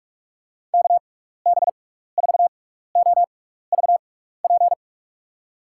Morse code, heard as a keyed tone, about 44 wpm, spaced as for 15 wpm, 700 Hz. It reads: KB4OVP
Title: USA Call Signs - 6 Characters - 6x Character Spacing - 45wpm